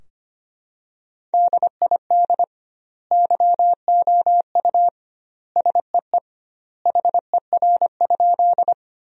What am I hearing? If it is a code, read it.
Morse code, heard as a keyed tone, about 25 wpm, 700 Hz. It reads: DID YOU SEE HER?
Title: 2x Word-Spacing - Sentences from Top 100 Words - 25wpm